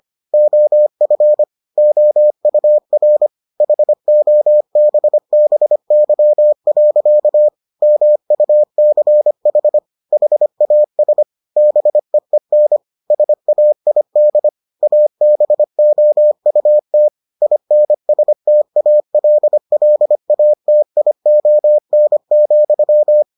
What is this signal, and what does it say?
Morse code, heard as a keyed tone, about 25 wpm, 600 Hz. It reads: OF OUR HOBBY. MUCH HAS BEEN SAID ABOUT INSTALLATION,